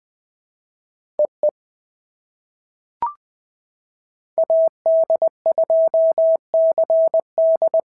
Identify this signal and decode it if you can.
Morse code, heard as a keyed tone, about 20 wpm, 650 Hz. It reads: EE AD2CD
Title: POTA QSO Practice - 20wpm